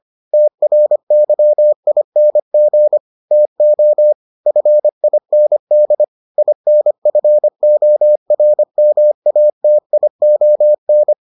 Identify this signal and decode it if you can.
Morse code, heard as a keyed tone, about 25 wpm, 600 Hz. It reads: TRYING TO FIND INFORMATION